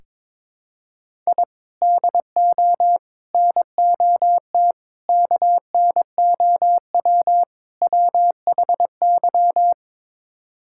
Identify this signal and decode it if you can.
Morse code, heard as a keyed tone, about 22 wpm, 700 Hz. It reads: I DO NOT KNOW WHY
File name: practice / 22wpm / Build A Phrase - 5-Word Phrases - 22wpm